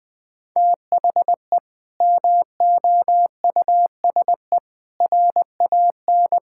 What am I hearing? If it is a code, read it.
Morse code, heard as a keyed tone, about 21 wpm, 700 Hz. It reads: THE MOUSE RAN
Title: Sentences from Top 1000 Words - Review Part 5 - 20wpm